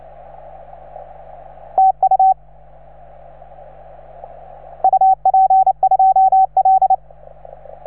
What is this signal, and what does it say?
Morse code, heard as a keyed tone, about 28 wpm, 750 Hz. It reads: TU UP2L